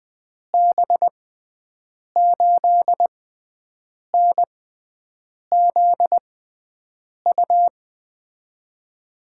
Morse code, heard as a keyed tone, about 20 wpm, spaced as for 7 wpm, 700 Hz. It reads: B8NZU
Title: USA Call Signs - 6 Characters - 6x Character Spacing - 20wpm